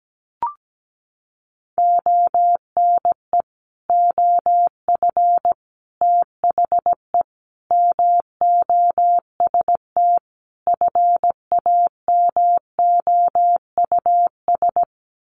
Morse code, heard as a keyed tone, about 17 wpm, 700 Hz. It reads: ONE OF THE MOST FAMOUS
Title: Sets of 5 Words Encore2 17wpm